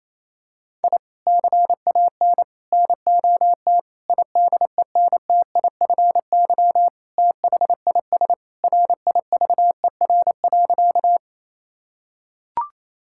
Morse code, heard as a keyed tone, about 27 wpm, 700 Hz. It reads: I CAN NOT IDENTIFY THIS RIVER.
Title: Sentences from Top 1000 Words - Review Part 6 - 28wpm